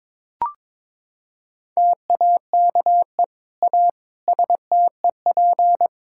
Morse code, heard as a keyed tone, about 22 wpm, 700 Hz. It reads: TAKE A STEP
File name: Sentences with 7-Word Phrases - 22wpm